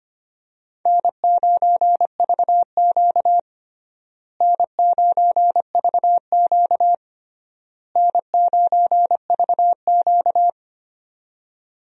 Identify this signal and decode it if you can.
Morse code, heard as a keyed tone, about 25 wpm, 700 Hz. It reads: N9VQ N9VQ N9VQ
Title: USA Call Signs - 4 Characters - Repeated 3x - 25wpm